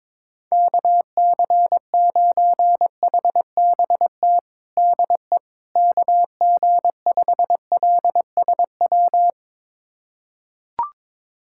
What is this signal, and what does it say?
Morse code, heard as a keyed tone, about 21 wpm, 700 Hz. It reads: KC9HBT DE KG5LSW
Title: USA Callsign DE Callsign 22wpm